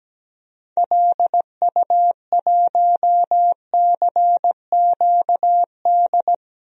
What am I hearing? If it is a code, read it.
Morse code, heard as a keyed tone, about 17 wpm, 700 Hz. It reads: LU1CQD
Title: DXCC-100 Call Signs - 6 Characters - 17wpm